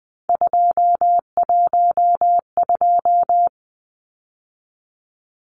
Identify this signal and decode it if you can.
Morse code, heard as a keyed tone, about 19 wpm, 700 Hz. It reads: 212
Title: Numbers - 3-Digit 20wpm